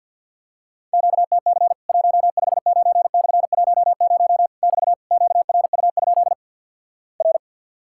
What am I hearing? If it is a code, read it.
Morse code, heard as a keyed tone, about 48 wpm, 700 Hz. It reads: QTC 159/10 <BT> QRU? R
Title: Worked All Europe DX Contest - QTC Exchange with Repeats - 50wpm